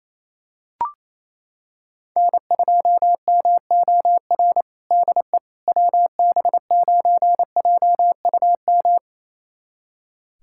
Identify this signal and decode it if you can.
Morse code, heard as a keyed tone, about 28 wpm, 700 Hz. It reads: N2MOR DE WB9JUM
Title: USA Callsign DE Callsign 28wpm